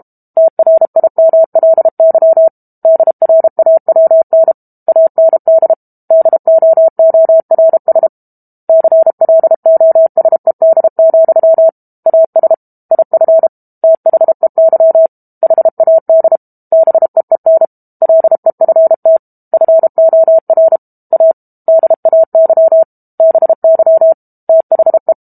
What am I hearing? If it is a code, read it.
Morse code, heard as a keyed tone, about 32 wpm, 650 Hz. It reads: TRIMLY DRAWN AND DOORS CLOSED, AS IF THEY HAD BEEN LEFT FOR A DAY BY THE
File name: Book Two / 06-Chapter 6 - 32 WPM